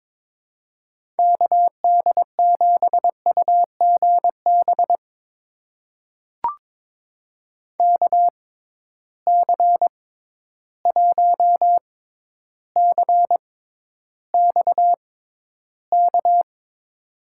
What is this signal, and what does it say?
Morse code, heard as a keyed tone, about 22 wpm, spaced as for 11 wpm, 700 Hz. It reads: KD7UGB KC1CXK
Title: USA Call Signs - 6 Characters - 6x Character Spacing - 22wpm